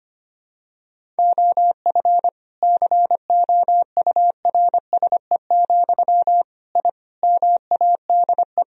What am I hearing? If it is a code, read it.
Morse code, heard as a keyed tone, about 25 wpm, 700 Hz. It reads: OF COURSE, I MADE
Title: Sentences from Top 1000 Words - Review Part 1 - 25wpm